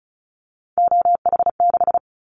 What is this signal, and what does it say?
Morse code, heard as a keyed tone, about 34 wpm, 700 Hz. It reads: OH6
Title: Callsign Prefixes for Top 50 DXCC - 35wpm